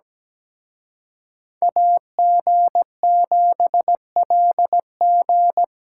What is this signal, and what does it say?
Morse code, heard as a keyed tone, about 20 wpm, 700 Hz. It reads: AG7LG
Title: Speed Racing - USA 2x2 Call Signs 17wpm